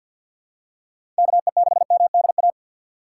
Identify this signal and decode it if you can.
Morse code, heard as a keyed tone, about 50 wpm, 700 Hz. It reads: KE6MDA
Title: USA Call Signs - 2x Character Spacing - 50wpm